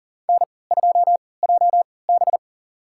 Morse code, heard as a keyed tone, about 37 wpm, spaced as for 22 wpm, 700 Hz. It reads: N2JB
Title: USA Call Signs - 4 Characters - 3x Character Spacing - 40wpm